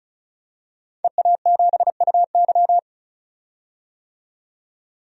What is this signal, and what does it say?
Morse code, heard as a keyed tone, about 36 wpm, 700 Hz. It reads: EA7UY
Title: DXCC-100 Call Signs - 5 Characters - 35wpm